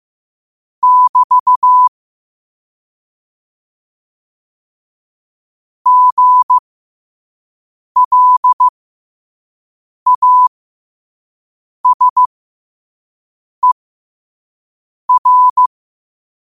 Morse code, heard as a keyed tone, about 15 wpm, spaced as for 5 wpm, 1000 Hz. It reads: <BT> GLASER
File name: qotdfiles / 5wpm